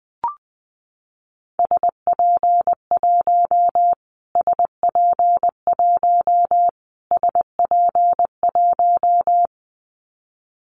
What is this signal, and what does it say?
Morse code, heard as a keyed tone, about 20 wpm, 700 Hz. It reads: SP1 SP1 SP1
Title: Callsign Prefixes for Top 25 DXCC - Repeated 3x - 20wpm